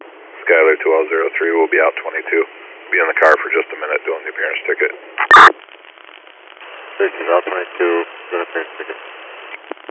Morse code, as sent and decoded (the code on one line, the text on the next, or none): none